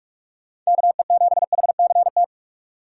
Morse code, heard as a keyed tone, about 44 wpm, 700 Hz. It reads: KE7HCT